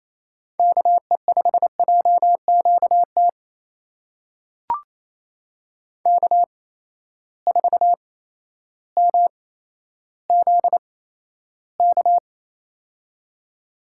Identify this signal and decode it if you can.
Morse code, heard as a keyed tone, about 28 wpm, spaced as for 13 wpm, 700 Hz. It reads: KE5JQT K4MZK
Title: USA Call Signs - 8x Character Spacing - 28wpm